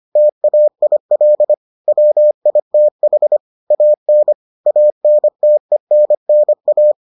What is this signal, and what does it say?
Morse code, heard as a keyed tone, about 25 wpm, 600 Hz. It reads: TAIL WITH AN ANTENNA